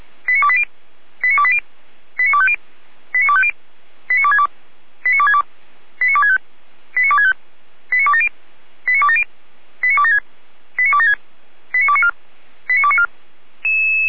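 Five-tone selective call (ZVEI), heard as quick runs of five tones, tones of about 70 ms. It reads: 89280 89280 89250 89250 89262 89262 89265 89265 89270 89270 89276 89276 89294 89294
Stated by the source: ZVEI five-tone selective call received by SDR.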